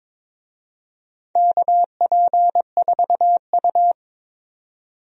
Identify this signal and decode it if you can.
Morse code, heard as a keyed tone, about 23 wpm, 700 Hz. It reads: KP4U
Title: DXCC-100 Call Signs - 4 Characters - 22wpm